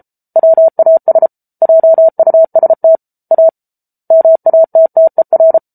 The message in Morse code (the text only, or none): WAS JUST A MATTER